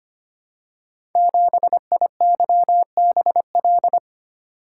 Morse code, heard as a keyed tone, about 25 wpm, 700 Hz. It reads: 7IYBL
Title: Code Groups - 5 Letter:Numbers - 25wpm